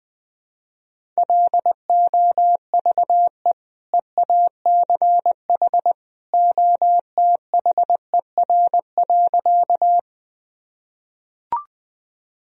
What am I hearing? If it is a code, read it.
Morse code, heard as a keyed tone, about 20 wpm, 700 Hz. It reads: LOVE EACH OTHER.
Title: Sentences from Top 1000 Words - Review Part 1 - 20wpm